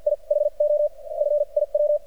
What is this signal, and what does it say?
Morse code, heard as a keyed tone, about 50 wpm, spaced as for 41 wpm, 590 Hz. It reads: I HO F IW
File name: archived / CRSnet